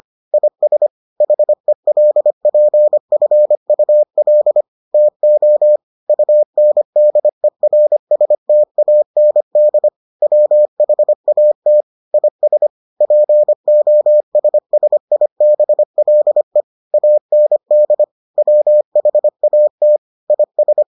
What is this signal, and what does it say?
Morse code, heard as a keyed tone, about 25 wpm, 600 Hz. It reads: IS HELPFUL TO UNDERSTAND WHAT IS POSSIBLE AND WHAT IS